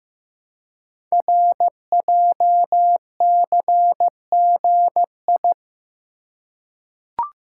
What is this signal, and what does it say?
Morse code, heard as a keyed tone, about 15 wpm, 700 Hz. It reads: RJCGI
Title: Code Groups - 5 Letters - 15wpm